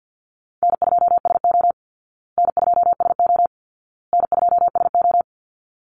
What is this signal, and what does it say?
Morse code, heard as a keyed tone, about 49 wpm, 700 Hz. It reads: N2SO N2SO N2SO
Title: Call Signs Repeated 2x - 50wpm